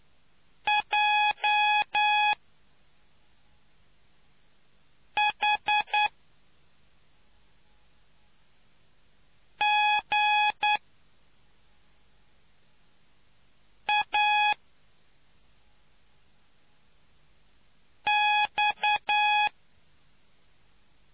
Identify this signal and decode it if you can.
Morse code, heard as a keyed tone, about 9 wpm, 840 Hz. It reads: J H G A X